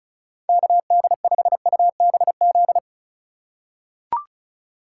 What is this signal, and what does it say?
Morse code, heard as a keyed tone, about 36 wpm, 700 Hz. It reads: KD5UBZ
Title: USA Call Signs - 6 Characters - 4x Character Spacing - 35wpm